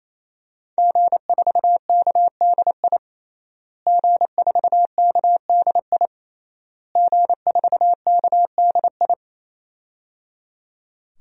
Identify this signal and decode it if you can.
Morse code, heard as a keyed tone, about 28 wpm, 700 Hz. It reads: G4KDI G4KDI G4KDI
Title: DXCC-100 Call Signs - 5 Characters - Repeated 3x - 28wpm